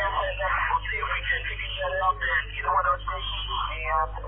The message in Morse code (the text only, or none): none